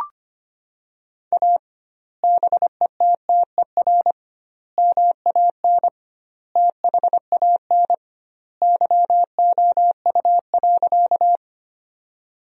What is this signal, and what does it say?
Morse code, heard as a keyed tone, about 25 wpm, 700 Hz. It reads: A BETTER MAN THAN YOU.
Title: 2x Word-Spacing - Sentences from Top 400 Words - 25wpm